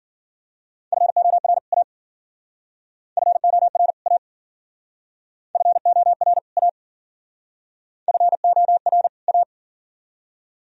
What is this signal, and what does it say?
Morse code, heard as a keyed tone, about 49 wpm, 700 Hz. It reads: FOR A FOR A FOR A FOR A
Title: Speed-Racing - Sets of 2 Words 40wpm